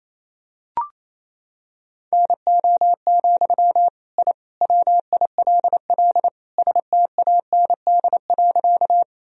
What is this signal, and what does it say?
Morse code, heard as a keyed tone, about 28 wpm, 700 Hz. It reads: NO, I WILL STAND.
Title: Sentences from Top 1000 Words - Review Part 1 - 28wpm